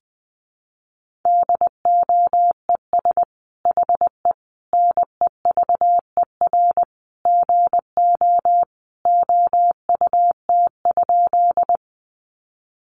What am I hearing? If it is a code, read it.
Morse code, heard as a keyed tone, about 19 wpm, 700 Hz. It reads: DOES HE NEVER GO OUT?
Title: Sentences from Top 300 Words 20wpm